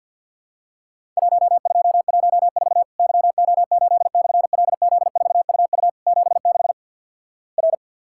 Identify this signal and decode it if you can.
Morse code, heard as a keyed tone, about 50 wpm, 700 Hz. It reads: 1214 YO8/LZ4UU 76 R
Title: Worked All Europe DX Contest - QTC Exchange - 50wpm